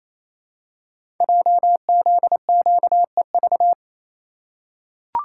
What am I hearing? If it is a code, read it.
Morse code, heard as a keyed tone, about 29 wpm, 700 Hz. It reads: JZQEV
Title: Code Groups - 5 Letters - 28wpm